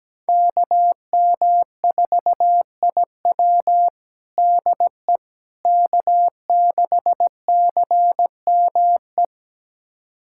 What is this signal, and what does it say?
Morse code, heard as a keyed tone, about 17 wpm, 700 Hz. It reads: KM4IW DE K6CME
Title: USA Callsign DE Callsign - 5 Characters - 17wpm